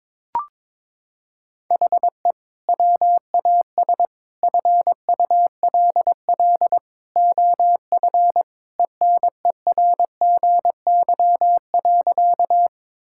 Morse code, heard as a keyed tone, about 22 wpm, 700 Hz. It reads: HE WAS FULL OF ENERGY.
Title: Sentences from Top 1000 Words - Review Part 6 - 22wpm